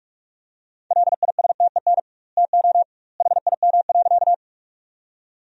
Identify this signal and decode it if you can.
Morse code, heard as a keyed tone, about 46 wpm, 700 Hz. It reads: LISTEN TO HIM.
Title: 2x Word-Spacing - Sentences from Top 400 Words - 45wpm